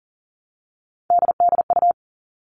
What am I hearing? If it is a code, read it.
Morse code, heard as a keyed tone, about 40 wpm, 700 Hz. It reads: DDU
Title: All Cut Numbers - 3-Digit 40wpm